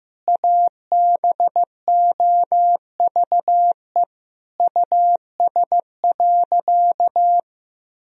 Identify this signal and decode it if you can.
Morse code, heard as a keyed tone, about 15 wpm, 700 Hz. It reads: ABOVE US.
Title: Sentences from Top 1000 Words - Review Part 2 - 15wpm